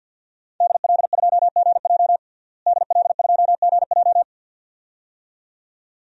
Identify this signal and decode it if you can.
Morse code, heard as a keyed tone, about 48 wpm, 700 Hz. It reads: DL2GJ DL2GJ
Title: DXCC-100 Call Signs - 5 Characters - Repeated 2x - 50wpm